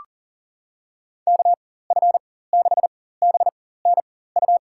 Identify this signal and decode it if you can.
Morse code, heard as a keyed tone, about 37 wpm, spaced as for 18 wpm, 700 Hz. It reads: KF6BNU
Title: USA Call Signs - 6 Characters - 4x Character Spacing - 40wpm